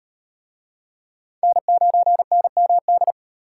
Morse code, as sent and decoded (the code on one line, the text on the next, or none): -. ----. -. -- -..
N9NMD